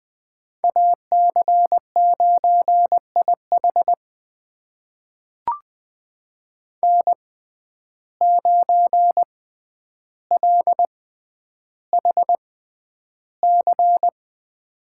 Morse code, heard as a keyed tone, about 21 wpm, spaced as for 10 wpm, 700 Hz. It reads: AC9IH N9LHC
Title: USA Call Signs - 6x Character Spacing - 20wpm